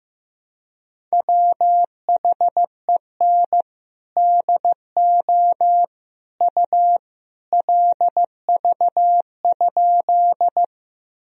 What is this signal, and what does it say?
Morse code, heard as a keyed tone, about 15 wpm, 700 Hz. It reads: WHEN DO U LV?
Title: Sentences from Top 400 Words and Abbreviations 15wpm